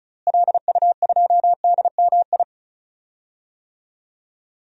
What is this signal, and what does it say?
Morse code, heard as a keyed tone, about 36 wpm, 700 Hz. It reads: LU2DMI
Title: DXCC-100 Call Signs - 6 Characters - Repeated 3x - 35wpm